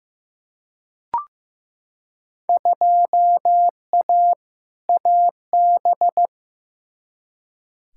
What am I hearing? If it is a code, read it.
Morse code, heard as a keyed tone, about 15 wpm, 700 Hz. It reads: 2A AB
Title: Field Day Exchanges - 15wpm